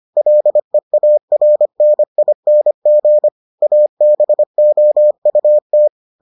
Morse code, heard as a keyed tone, about 25 wpm, 600 Hz. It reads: LEARNING ABOUT